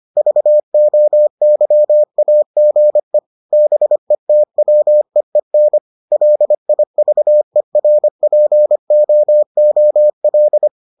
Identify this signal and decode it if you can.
Morse code, heard as a keyed tone, about 25 wpm, 600 Hz. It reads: VOYAGE BETWEEN LIVERPOOL